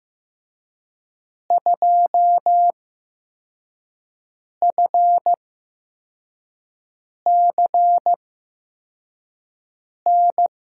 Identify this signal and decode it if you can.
Morse code, heard as a keyed tone, about 15 wpm, spaced as for 4 wpm, 700 Hz. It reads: 2FCN
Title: USA Call Signs - 8x Character Spacing - 15wpm